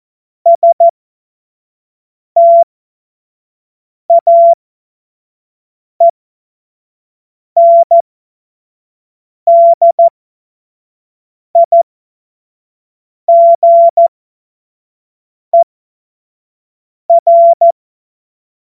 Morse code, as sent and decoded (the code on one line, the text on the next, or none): ... - .- . -. -.. .. --. .-.
STAENDIGER